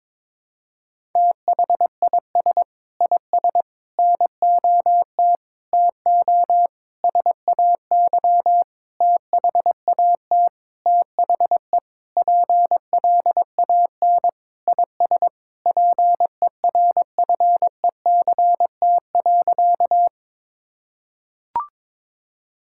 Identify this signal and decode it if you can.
Morse code, heard as a keyed tone, about 22 wpm, 700 Hz. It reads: THIS IS NOT TO SAY THAT THE PLAN IS PERFECT.
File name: Sentences with 7-Word Phrases - 22wpm